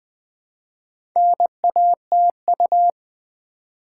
Morse code, heard as a keyed tone, about 20 wpm, 700 Hz. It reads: NATU
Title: Build A Word - 2 Letters At A Time - 6-Letter Words - 20wpm